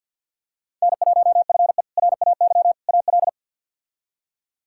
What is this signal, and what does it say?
Morse code, heard as a keyed tone, about 51 wpm, 700 Hz. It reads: N1FE RAY AL
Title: K1USN SST QSO Practice - 50wpm